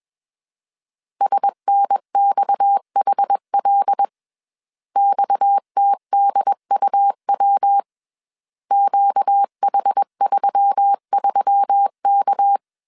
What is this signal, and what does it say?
Morse code, heard as a keyed tone, about 21 wpm, 800 Hz. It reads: SNXHL XTDUW QH33K
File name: CW / DET / 1 6 1500C S C 20wpm 2116